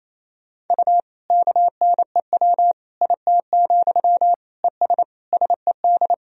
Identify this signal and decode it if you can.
Morse code, heard as a keyed tone, about 28 wpm, 700 Hz. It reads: U KNEW IT, ES SED
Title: Sentences from Top 500 Words and Abbreviations 28wpm